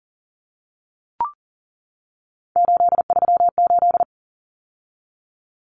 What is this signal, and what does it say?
Morse code, heard as a keyed tone, about 40 wpm, 700 Hz. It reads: 838
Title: Numbers - 3-Digit 40wpm